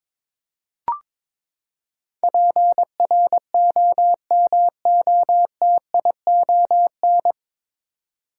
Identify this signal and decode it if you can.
Morse code, heard as a keyed tone, about 22 wpm, 700 Hz. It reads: PROMOTION
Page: Morse Code Ninja practice 22wpm